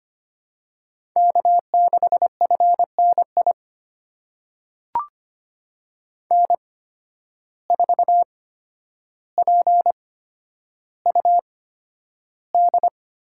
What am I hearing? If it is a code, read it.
Morse code, heard as a keyed tone, about 25 wpm, spaced as for 11 wpm, 700 Hz. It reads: K6FNI N4PUD
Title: USA Call Signs - 5 Characters - 8x Character Spacing - 25wpm